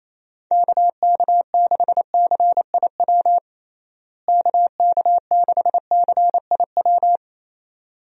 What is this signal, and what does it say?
Morse code, heard as a keyed tone, about 28 wpm, 700 Hz. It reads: KK6CIW KK6CIW
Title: USA Call Signs - 6 Characters - Repeated 2x - 28wpm